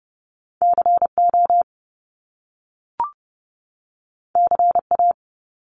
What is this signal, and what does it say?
Morse code, heard as a keyed tone, about 30 wpm, 700 Hz. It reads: CO CA
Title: US States 30wpm